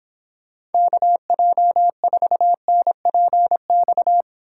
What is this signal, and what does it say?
Morse code, heard as a keyed tone, about 31 wpm, 700 Hz. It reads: KJ4NPX